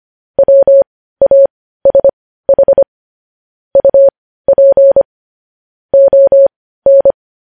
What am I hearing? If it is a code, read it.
Morse code, heard as a keyed tone, about 25 wpm, spaced as for 15 wpm, 550 Hz. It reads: WASH UP ON
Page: news-headlines-in-morse-code-at-15-wpm